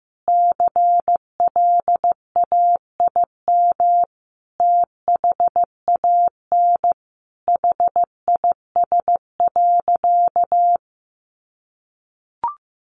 Morse code, heard as a keyed tone, about 15 wpm, 700 Hz. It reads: CLAIM THAN HIS.
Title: Sentences from Top 1500 Words 15wpm